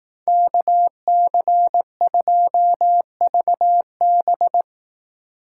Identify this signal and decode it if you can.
Morse code, heard as a keyed tone, about 18 wpm, 700 Hz. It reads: KC2VB
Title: Speed Racing - USA 2x2 Call Signs 15wpm